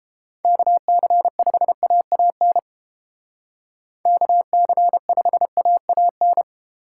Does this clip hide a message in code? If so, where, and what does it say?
Yes: Morse code from 0.4 to 6.4 s, heard as a keyed tone, about 29 wpm, 700 Hz. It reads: KC5AAN KC5AAN